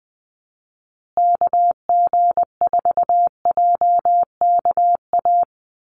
Morse code, heard as a keyed tone, about 20 wpm, 700 Hz. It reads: KG4JKA
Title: USA Call Signs Repeated 2x 20wpm